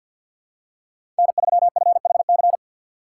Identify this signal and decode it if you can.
Morse code, heard as a keyed tone, about 53 wpm, 700 Hz. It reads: N3FHC
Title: USA Call Signs - 5 Characters - 4x Character Spacing - 50wpm